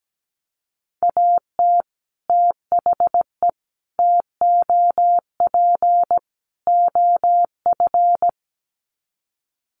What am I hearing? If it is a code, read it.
Morse code, heard as a keyed tone, about 17 wpm, 700 Hz. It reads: AT THE TOP OF
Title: Sets of 4 Words 17wpm